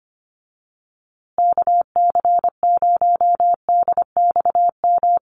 Morse code, heard as a keyed tone, about 25 wpm, 700 Hz. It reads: KC0DXM
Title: USA Call Signs Repeated 2x 25wpm